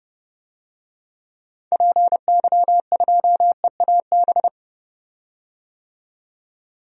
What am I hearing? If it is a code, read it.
Morse code, heard as a keyed tone, about 29 wpm, 690 Hz. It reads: PY2EAB
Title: DXCC-100 Call Signs - 6 Characters - 30wpm